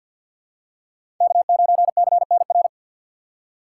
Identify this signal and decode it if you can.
Morse code, heard as a keyed tone, about 52 wpm, 700 Hz. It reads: K9CNR
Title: USA Call Signs - 5 Characters - 3x Character Spacing - 50wpm